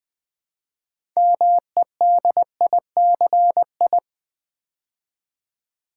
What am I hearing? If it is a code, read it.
Morse code, heard as a keyed tone, about 20 wpm, 700 Hz. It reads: MEDICI
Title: Build A Word - 3 Letters At A Time - 8-Letter Words - 20wpm